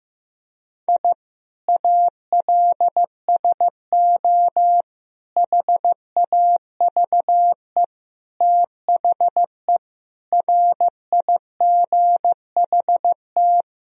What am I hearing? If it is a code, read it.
Morse code, heard as a keyed tone, about 15 wpm, 700 Hz. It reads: I ALSO HAVE THE RIGHT
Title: Sentences from Top 1000 Words - Review Part 1 - 15wpm